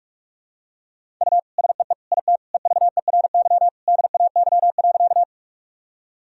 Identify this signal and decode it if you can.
Morse code, heard as a keyed tone, about 41 wpm, 700 Hz. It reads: U SEE IT EVERY DAY.
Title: Sentences from Top 200 Words and Abbreviations 45wpm